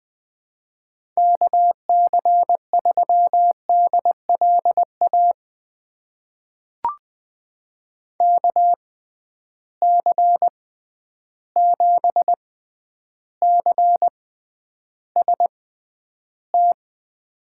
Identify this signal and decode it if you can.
Morse code, heard as a keyed tone, about 20 wpm, spaced as for 10 wpm, 700 Hz. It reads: KC3DLA KC7CST
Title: USA Call Signs - 6 Characters - 6x Character Spacing - 20wpm